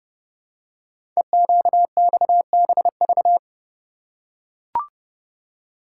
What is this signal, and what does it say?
Morse code, heard as a keyed tone, about 30 wpm, 700 Hz. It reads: EQXBV
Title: Code Groups - 5 Letter:Numbers - 30wpm